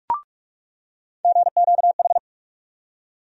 Morse code, heard as a keyed tone, about 46 wpm, 700 Hz. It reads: GQH
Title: USA Call Sign Postfixes - 3 Letters 45wpm